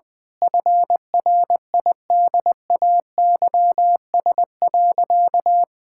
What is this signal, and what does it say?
Morse code, heard as a keyed tone, about 20 wpm, 700 Hz. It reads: FRIDAYS.
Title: Difficult Sentences with Hints - 20wpm